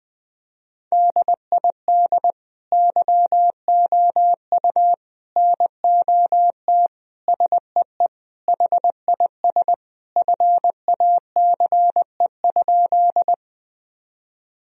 Morse code, heard as a keyed tone, about 20 wpm, 700 Hz. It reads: DID YOU NOT SEE HIS FACE?
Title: Sentences from Top 1000 Words - Review Part 2 - 20wpm